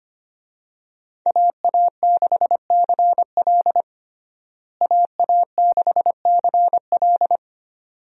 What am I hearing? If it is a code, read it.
Morse code, heard as a keyed tone, about 25 wpm, 700 Hz. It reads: AA6CL AA6CL
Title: USA Call Signs - 5 Characters - Repeated 2x - 25wpm